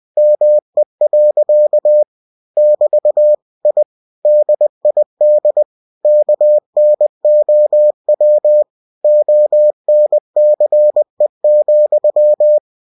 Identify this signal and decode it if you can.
Morse code, heard as a keyed tone, about 20 wpm, 600 Hz. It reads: ME. <BT> I DID KNOW ONCE,
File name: cw-books / winnie-the-pooh / winnie-20wpm-page-0084